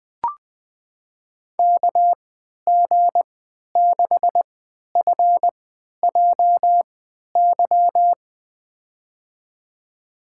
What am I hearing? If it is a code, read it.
Morse code, heard as a keyed tone, about 20 wpm, spaced as for 11 wpm, 700 Hz. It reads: KG6FJY
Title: USA Call Signs - 6 Characters - 3x Character Spacing - 20wpm